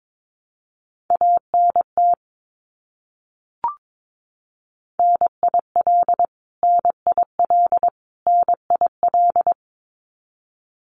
Morse code, heard as a keyed tone, about 22 wpm, 700 Hz. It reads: ANT NIL NIL NIL
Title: Abbreviations Repeated 3x - 22wpm